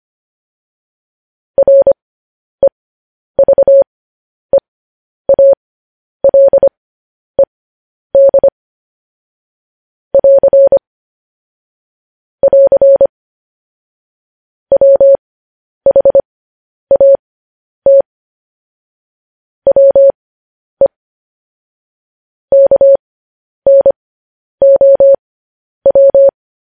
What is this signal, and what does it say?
Morse code, heard as a keyed tone, about 25 wpm, spaced as for 10 wpm, 550 Hz. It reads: REVEALED <AR> <AR> WHAT WE KNOW
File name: podcast / News headlines-1x-25x10